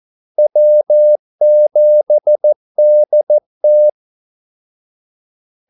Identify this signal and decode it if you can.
Morse code, heard as a keyed tone, about 14 wpm, 600 Hz. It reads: W7DT
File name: Callsigns-14wpm